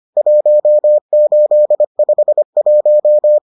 Morse code, heard as a keyed tone, about 25 wpm, 600 Hz. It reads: 1851